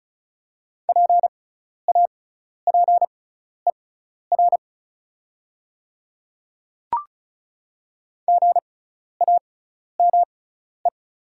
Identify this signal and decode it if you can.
Morse code, heard as a keyed tone, about 36 wpm, spaced as for 12 wpm, 700 Hz. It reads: PAPER GAME
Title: 6x Character Spacing - Top 500 Words - 35wpm